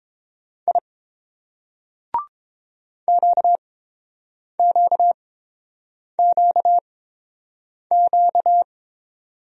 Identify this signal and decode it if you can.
Morse code, heard as a keyed tone, about 29 wpm, spaced as for 7 wpm, 690 Hz. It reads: I QQQQ